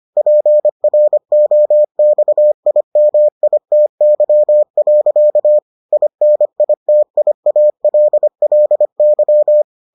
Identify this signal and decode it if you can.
Morse code, heard as a keyed tone, about 25 wpm, 600 Hz. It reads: PROXIMITY. INITIALLY